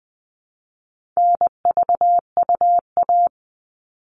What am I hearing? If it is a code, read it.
Morse code, heard as a keyed tone, about 20 wpm, 700 Hz. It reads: NVUA
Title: All Cut Numbers - 4-Digit 20wpm